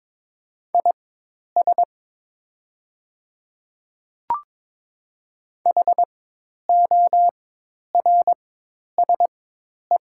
Morse code, heard as a keyed tone, about 22 wpm, spaced as for 10 wpm, 700 Hz. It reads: IS HORSE
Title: 4x Character Spacing - Top 300 Words - 22wpm